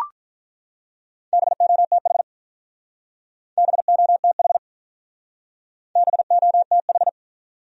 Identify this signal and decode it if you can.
Morse code, heard as a keyed tone, about 43 wpm, 700 Hz. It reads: BOTH BOTH BOTH
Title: Speed-Racing Top 500 Words 35wpm